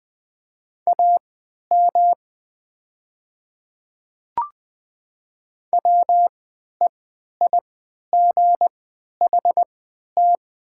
Morse code, heard as a keyed tone, about 20 wpm, spaced as for 11 wpm, 700 Hz. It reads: AM WEIGHT
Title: 3x Character Spacing - Top 500 Words - 20wpm